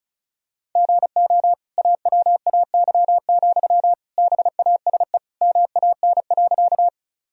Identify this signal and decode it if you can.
Morse code, heard as a keyed tone, about 34 wpm, 700 Hz. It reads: GO AWAY, BASE MAN.
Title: Sentences from Top 1000 Words - Review Part 1 - 35wpm